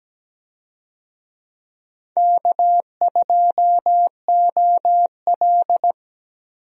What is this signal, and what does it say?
Morse code, heard as a keyed tone, about 17 wpm, 700 Hz. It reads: K2OL
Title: USA Call Signs - 4 Characters - Repeated 3x - 17wpm